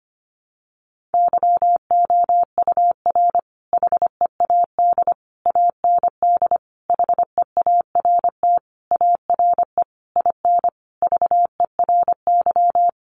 Morse code, heard as a keyed tone, about 25 wpm, 700 Hz. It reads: YOUR HEAD AND HEART ARE IN VERY
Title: Sentences from Top 1300 Words 25wpm